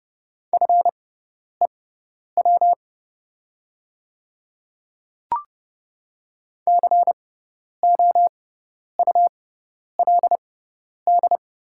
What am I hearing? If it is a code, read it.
Morse code, heard as a keyed tone, about 30 wpm, spaced as for 10 wpm, 700 Hz. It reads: FEW COULD
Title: 6x Character Spacing - Top 500 Words - 30wpm